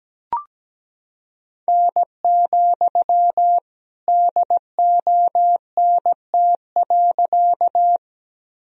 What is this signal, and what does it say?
Morse code, heard as a keyed tone, about 17 wpm, 700 Hz. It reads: N, DONT.
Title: Sentences from Top 300 Words and Abbreviations 17wpm